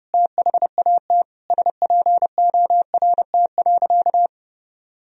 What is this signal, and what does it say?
Morse code, heard as a keyed tone, about 30 wpm, 700 Hz. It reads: THAT SPORT.
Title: Sentences from Top 1000 Words - Review Part 4 - 30wpm